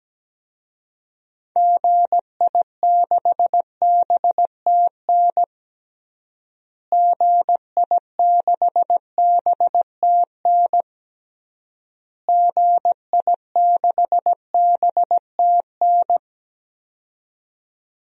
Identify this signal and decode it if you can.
Morse code, heard as a keyed tone, about 17 wpm, 700 Hz. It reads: GI6BTN GI6BTN GI6BTN
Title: DXCC-100 Call Signs - 6 Characters - Repeated 3x - 17wpm